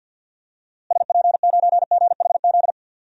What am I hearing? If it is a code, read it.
Morse code, heard as a keyed tone, about 53 wpm, 700 Hz. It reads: SP9GHZ
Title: DXCC-100 Call Signs - 6 Characters - Repeated 2x - 50wpm